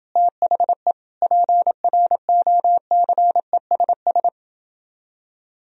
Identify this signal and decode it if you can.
Morse code, heard as a keyed tone, about 29 wpm, 700 Hz. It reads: THE PROCESS